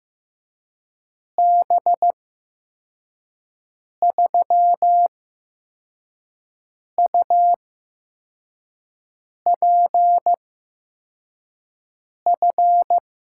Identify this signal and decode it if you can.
Morse code, heard as a keyed tone, about 15 wpm, spaced as for 4 wpm, 700 Hz. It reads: B3UPF
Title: USA Call Signs - 6 Characters - 8x Character Spacing - 15wpm